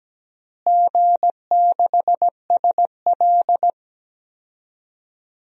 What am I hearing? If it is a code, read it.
Morse code, heard as a keyed tone, about 17 wpm, 700 Hz. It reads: G6SL
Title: DXCC-100 Call Signs - 4 Characters - Repeated 2x - 17wpm